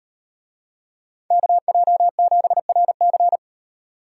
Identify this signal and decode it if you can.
Morse code, heard as a keyed tone, about 38 wpm, 700 Hz. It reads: KJ7RC